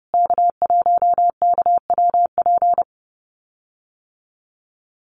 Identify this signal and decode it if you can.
Morse code, heard as a keyed tone, about 30 wpm, 700 Hz. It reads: K1KWP
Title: USA Call Signs Repeated 3x 30wpm